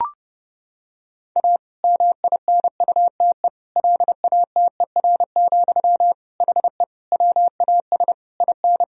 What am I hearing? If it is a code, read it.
Morse code, heard as a keyed tone, about 32 wpm, 700 Hz. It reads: A MINUTE LATER, HE WAS IN
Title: Sentences from Top 1000 Words - Review Part 4 - 30wpm